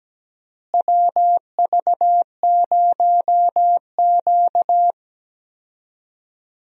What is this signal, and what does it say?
Morse code, heard as a keyed tone, about 17 wpm, 700 Hz. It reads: WV0Q
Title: USA Call Signs - 4 Characters - Repeated 3x - 17wpm